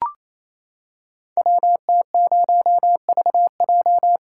Morse code, heard as a keyed tone, about 27 wpm, 700 Hz. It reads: WT0VJ
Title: Code Groups - 5 Letter:Numbers - 28wpm